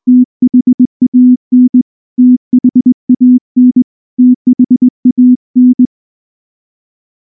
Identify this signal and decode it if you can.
Morse code, heard as a keyed tone, about 19 wpm, 250 Hz. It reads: THAN THAN THAN
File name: morse / morse-training-20WPM-4